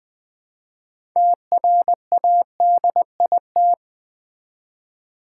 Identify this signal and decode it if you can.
Morse code, heard as a keyed tone, about 20 wpm, 700 Hz. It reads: TRADIT
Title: Build A Word - 3 Letters At A Time - 9-Letter Words - 20wpm